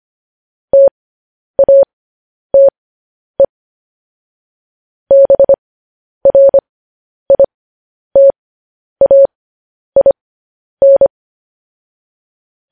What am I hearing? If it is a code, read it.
Morse code, heard as a keyed tone, about 25 wpm, spaced as for 10 wpm, 550 Hz. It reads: TATE BRITAIN
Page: news-headlines-in-morse-code-at-10-wpm